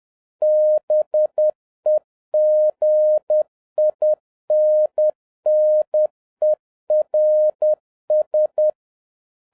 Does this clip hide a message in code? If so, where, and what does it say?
Yes: Morse code from 0.4 to 8.7 s, heard as a keyed tone, about 10 wpm, 610 Hz. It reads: BEGINNERS